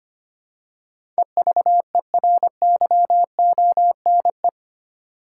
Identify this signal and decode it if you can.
Morse code, heard as a keyed tone, about 25 wpm, 700 Hz. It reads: EVERYONE
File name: 8 Letter Words - First 3 Letters Given - 25wpm